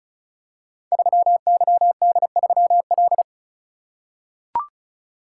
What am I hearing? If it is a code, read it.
Morse code, heard as a keyed tone, about 36 wpm, 700 Hz. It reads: 3YD3L
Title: Code Groups - 5 Letter:Numbers - 35wpm